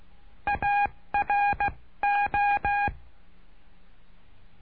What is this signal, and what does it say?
Morse code, heard as a keyed tone, about 16 wpm, 810 Hz. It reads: A R O
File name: cursos / CURSO DE TELEGRAFIA / Telegrafia - Trilha 09